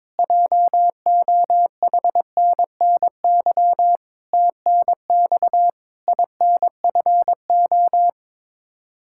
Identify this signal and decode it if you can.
Morse code, heard as a keyed tone, about 22 wpm, 700 Hz. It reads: JOHNNY TNX INFO
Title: Practice QSO - Exchange 3 - More Info - 22wpm